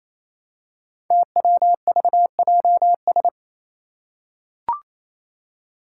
Morse code, heard as a keyed tone, about 27 wpm, 700 Hz. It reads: TWVJS
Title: Five Challenging Letter-Number Combinations 28wpm